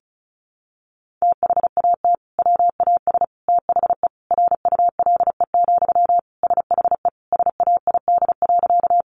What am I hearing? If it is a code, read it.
Morse code, heard as a keyed tone, about 34 wpm, 700 Hz. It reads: THAT WAS THE RULE, SHE SAID.
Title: Sentences from Top 400 Words 35wpm